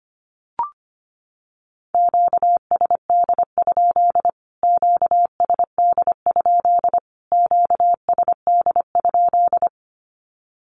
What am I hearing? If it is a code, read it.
Morse code, heard as a keyed tone, about 25 wpm, 700 Hz. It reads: QSD? QSD? QSD?
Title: Q-Codes Repeated 3x 25wpm